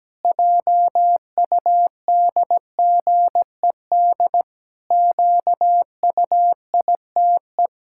Morse code, heard as a keyed tone, about 17 wpm, 700 Hz. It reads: JUDGED QUITE
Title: Pangram Challenge v2 - 17wpm